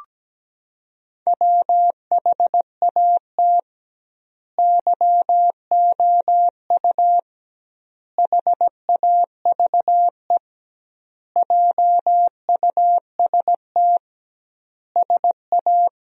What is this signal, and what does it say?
Morse code, heard as a keyed tone, about 17 wpm, 700 Hz. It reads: WHAT YOU HAVE JUST SA
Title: 2x Word-Spacing - Sentences from Top 200 Words - 17wpm